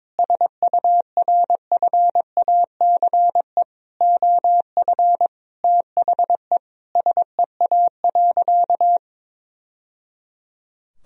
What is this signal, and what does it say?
Morse code, heard as a keyed tone, about 21 wpm, 700 Hz. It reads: SURFACE OF THE SEA.
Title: Sentences from Top 1000 Words - Review Part 3 - 22wpm